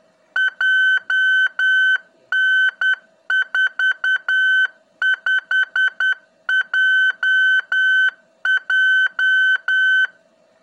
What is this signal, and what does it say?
Morse code, heard as a keyed tone, about 10 wpm, 1550 Hz. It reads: JN45JJ
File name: ik1hgi / utiliti / ik1hgi-qrss